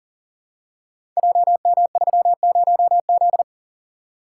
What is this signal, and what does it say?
Morse code, heard as a keyed tone, about 43 wpm, 690 Hz. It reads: JM30Z
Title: Code Groups - 5 Letter:Numbers - 40wpm